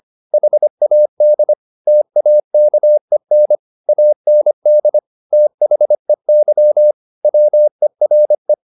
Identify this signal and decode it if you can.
Morse code, heard as a keyed tone, about 25 wpm, 600 Hz. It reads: HAD TAKEN AND THEY WERE
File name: the old man and the sea part 1 25wpm